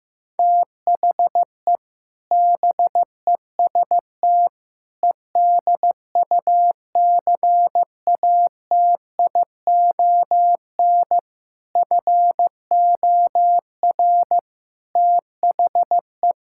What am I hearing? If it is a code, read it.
Morse code, heard as a keyed tone, about 15 wpm, 700 Hz. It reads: THE BEST EDUCATION FOR THE
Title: Sentences from Top 1000 Words - Review Part 6 - 15wpm